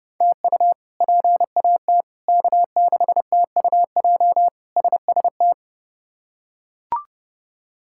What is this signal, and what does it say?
Morse code, heard as a keyed tone, about 30 wpm, 700 Hz. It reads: TU PAT K6TUJ SST
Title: K1USN SST QSO Practice - 30wpm